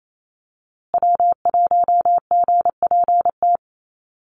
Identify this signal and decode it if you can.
Morse code, heard as a keyed tone, about 27 wpm, 700 Hz. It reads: W1GPT
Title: USA Call Signs Repeated 3x 28wpm